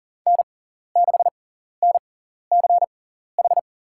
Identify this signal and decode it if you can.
Morse code, heard as a keyed tone, about 40 wpm, spaced as for 14 wpm, 700 Hz. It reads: N6NCH
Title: USA Call Signs - 5 Characters - 6x Character Spacing - 40wpm